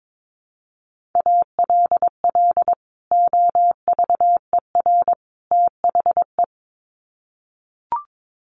Morse code, heard as a keyed tone, about 22 wpm, 700 Hz. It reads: ALL OVER THE
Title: Sets of 3 Words 22wpm